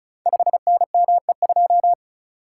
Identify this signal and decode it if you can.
Morse code, heard as a keyed tone, about 36 wpm, 700 Hz. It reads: 5NME2
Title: Code Groups - 5 Letter:Numbers - 35wpm